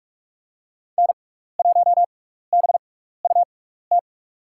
Morse code, heard as a keyed tone, about 48 wpm, spaced as for 16 wpm, 700 Hz. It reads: N1BUT
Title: USA Call Signs - 6x Character Spacing - 45wpm